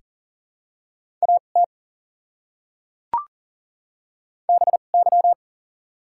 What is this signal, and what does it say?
Morse code, heard as a keyed tone, about 40 wpm, spaced as for 29 wpm, 700 Hz. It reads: AT BY
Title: 2x Character Spacing - Top 100 Words - 40wpm